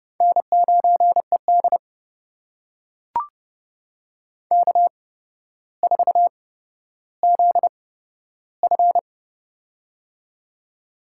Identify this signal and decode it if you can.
Morse code, heard as a keyed tone, about 32 wpm, spaced as for 13 wpm, 700 Hz. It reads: N9ED K4ZF
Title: USA Call Signs - 4 Characters - 8x Character Spacing - 30wpm